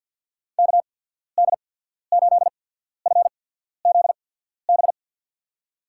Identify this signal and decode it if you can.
Morse code, heard as a keyed tone, about 53 wpm, spaced as for 14 wpm, 700 Hz. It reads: KD8FZB
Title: USA Call Signs - 6 Characters - 8x Character Spacing - 50wpm